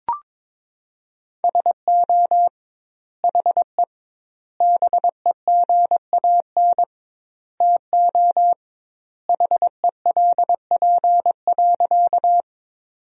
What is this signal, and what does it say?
Morse code, heard as a keyed tone, about 22 wpm, 700 Hz. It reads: SO HE BEGAN TO HELP.
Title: 2x Word-Spacing - Sentences from Top 300 Words - 22wpm